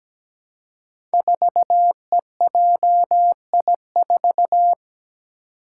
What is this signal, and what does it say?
Morse code, heard as a keyed tone, about 17 wpm, 700 Hz. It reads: 4EJI4
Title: Five Challenging Letter-Number Combinations 17wpm